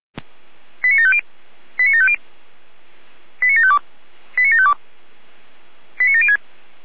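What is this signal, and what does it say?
Five-tone selective call (ZVEI), heard as quick runs of five tones, tones of about 80 ms. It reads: 89750 89750 89752 89752 89706
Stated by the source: ZVEI five-tone selective call received with an SDR, from the feuerwehr-orlamuende server.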